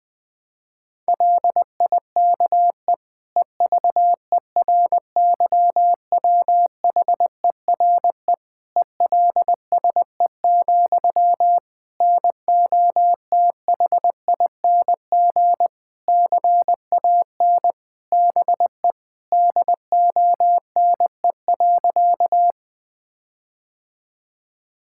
Morse code, heard as a keyed tone, about 21 wpm, 700 Hz. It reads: LIKE EVERYWHERE ELSE, NOTHING CAN BE DONE.
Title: Sentences from Top 1600 Words 20wpm